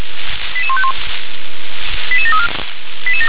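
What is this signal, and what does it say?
Five-tone selective call (ZVEI), heard as quick runs of five tones, tones of about 70 ms. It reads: 88151 88734
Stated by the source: ZVEI five-tone selective call received on an SDR.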